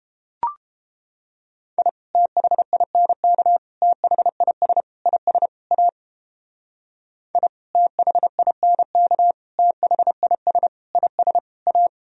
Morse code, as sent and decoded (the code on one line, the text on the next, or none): .. / - .... .. -. -.- / - .... .. ... / .. ... / .- / .. / - .... .. -. -.- / - .... .. ... / .. ... / .-
I THINK THIS IS A I THINK THIS IS A